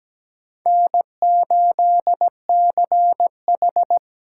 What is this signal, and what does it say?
Morse code, heard as a keyed tone, about 17 wpm, 700 Hz. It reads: N8CH
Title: USA Call Signs - 4 Characters - 6x Character Spacing - 17wpm